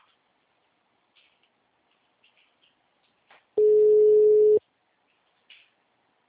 Ringback tone (Europe, 425 Hz).